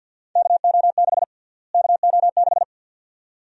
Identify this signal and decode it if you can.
Morse code, heard as a keyed tone, about 52 wpm, 700 Hz. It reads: KO6 KO6
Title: USA Call Sign Prefixes - Repeated 2x - 50wpm